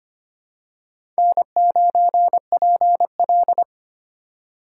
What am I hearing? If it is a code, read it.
Morse code, heard as a keyed tone, about 25 wpm, 700 Hz. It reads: N9PL
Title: USA Call Signs - 4 Characters - 2x Character Spacing - 25wpm